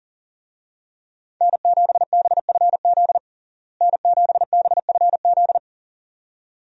Morse code, heard as a keyed tone, about 40 wpm, 700 Hz. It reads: N7BFZ N7BFZ
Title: USA Call Signs - 5 Characters - Repeated 2x - 40wpm